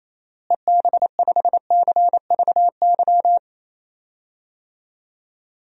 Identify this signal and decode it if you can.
Morse code, heard as a keyed tone, about 28 wpm, 700 Hz. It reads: EB5CVY
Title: DXCC-100 Call Signs - 6 Characters - Repeated 2x - 28wpm